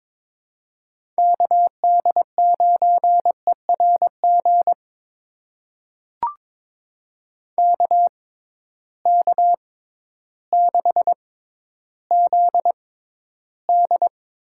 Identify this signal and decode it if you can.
Morse code, heard as a keyed tone, about 22 wpm, spaced as for 12 wpm, 700 Hz. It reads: KD9ERG KK6ZD
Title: USA Call Signs - 6 Characters - 6x Character Spacing - 22wpm